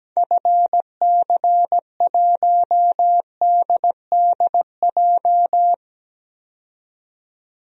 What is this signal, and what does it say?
Morse code, heard as a keyed tone, about 17 wpm, 700 Hz. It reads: FC1DDJ
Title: DXCC-100 Call Signs - 6 Characters - 17wpm